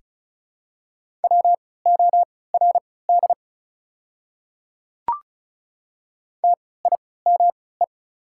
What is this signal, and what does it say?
Morse code, heard as a keyed tone, about 36 wpm, spaced as for 20 wpm, 700 Hz. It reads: WORD TIME
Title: 3x Character Spacing - Top 100 Words - 35wpm